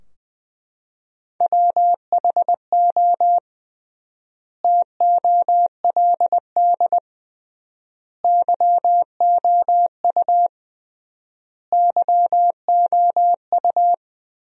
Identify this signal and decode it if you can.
Morse code, heard as a keyed tone, about 20 wpm, 700 Hz. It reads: WHO TOLD YOU YOU
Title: 3x Word-Spacing - Sentences from Top 400 Words - 20wpm